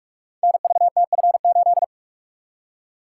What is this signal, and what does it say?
Morse code, heard as a keyed tone, about 44 wpm, 700 Hz. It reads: NVTF8
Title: Code Groups - 5 Letter:Numbers - 45wpm